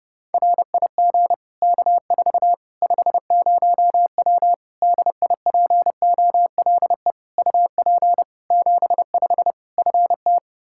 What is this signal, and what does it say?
Morse code, heard as a keyed tone, about 30 wpm, 700 Hz. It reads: RIG K4 50W DIPOLE UP 75 FT
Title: Practice QSO - Exchange 2 - Info - 30wpm